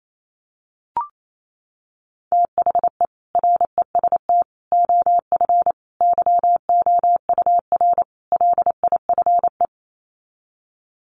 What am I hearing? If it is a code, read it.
Morse code, heard as a keyed tone, about 28 wpm, 700 Hz. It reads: THE REST OF YOUR LIFE
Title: Sets of 5 Words 28wpm